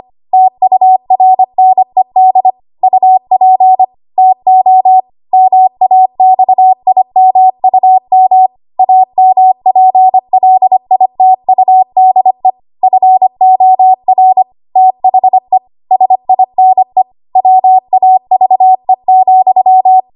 Morse code, heard as a keyed tone, about 25 wpm, 750 Hz. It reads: TURNED UP TO MAXIMUM AMPLITUDE FOR THE SINE WAVE,
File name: Archive / 25 WPM / 25WPM